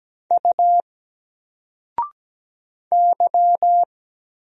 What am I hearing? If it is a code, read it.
Morse code, heard as a keyed tone, about 17 wpm, 700 Hz. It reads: U Y
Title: Single Character (Letter:Number:Punctuation:Prosign) 17wpm